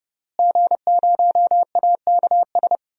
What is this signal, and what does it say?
Morse code, heard as a keyed tone, about 32 wpm, 700 Hz. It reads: G0AKS
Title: DXCC-100 Call Signs - 5 Characters - Repeated 2x - 30wpm